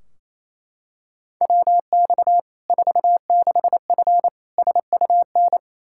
Morse code, heard as a keyed tone, about 28 wpm, 700 Hz. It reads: WX 46F SUN
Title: Practice QSO - Exchange 2 - Info - 28wpm